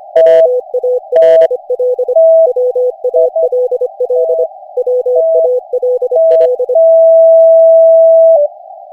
Morse code, heard as a keyed tone, about 25 wpm, 500 Hz. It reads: WALL WALL WALL
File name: cw / Session5 cw training